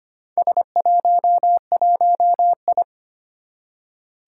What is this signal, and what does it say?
Morse code, heard as a keyed tone, about 25 wpm, 700 Hz. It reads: S11I
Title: Four Challenging Letter-Number Combinations 25wpm